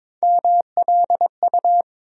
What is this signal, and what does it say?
Morse code, heard as a keyed tone, about 22 wpm, 700 Hz. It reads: MLU
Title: USA Call Sign Postfixes - 3 Letters 22wpm